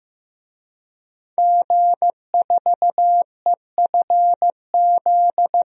Morse code, heard as a keyed tone, about 15 wpm, 700 Hz. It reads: G4EFZ
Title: DXCC-100 Call Signs - 5 Characters - Repeated 2x - 15wpm